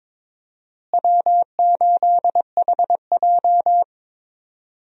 Morse code, heard as a keyed tone, about 22 wpm, 700 Hz. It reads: W8HJ
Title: USA Call Signs - 4 Characters - Repeated 2x - 22wpm